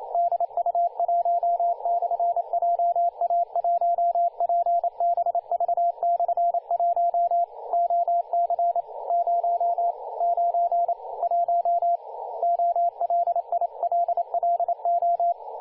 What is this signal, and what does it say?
Morse code, heard as a keyed tone, about 29 wpm, 690 Hz, with radio noise. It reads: DU1/JA1PBV/1 OC 091 OLILLO
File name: du1-ja1pbv-1-28-040mhz-cw-cqing